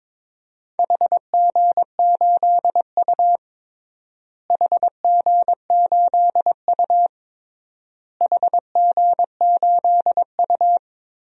Morse code, heard as a keyed tone, about 22 wpm, 700 Hz. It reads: HG8U HG8U HG8U
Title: DXCC-100 Call Signs - 4 Characters - Repeated 3x - 22wpm